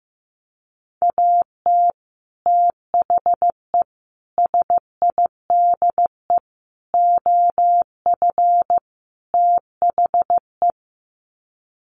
Morse code, heard as a keyed tone, about 14 wpm, 700 Hz. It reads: AT THE SIDE OF THE
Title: Sets of 5 Words Encore2 15wpm